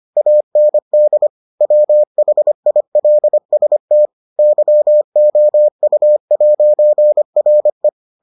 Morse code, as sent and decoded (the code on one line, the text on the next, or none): .- -. -.. / .-- .... .. .-.. ... - / -.-- --- ..- .----. .-. .
AND WHILST YOU'RE